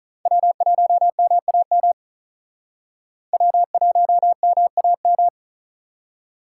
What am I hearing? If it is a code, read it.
Morse code, heard as a keyed tone, about 43 wpm, 700 Hz. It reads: W1MAM W1MAM
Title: Speed Racing - USA 1x3 Call Signs 35wpm